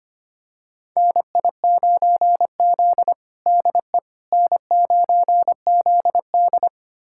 Morse code, heard as a keyed tone, about 25 wpm, 700 Hz. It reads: NI9Z DE N9ZD
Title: USA Callsign DE Callsign - 4 Characters - 25wpm